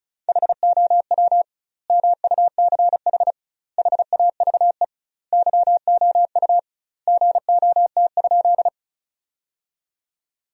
Morse code, heard as a keyed tone, about 34 wpm, 700 Hz. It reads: HOW MUCH HAVE YOU GOT?
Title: 2x Word-Spacing - Sentences from Top 300 Words - 35wpm